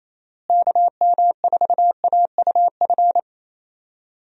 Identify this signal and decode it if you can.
Morse code, heard as a keyed tone, about 27 wpm, 700 Hz. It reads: KM4AUF
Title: USA Call Signs - 6 Characters - Repeated 3x - 28wpm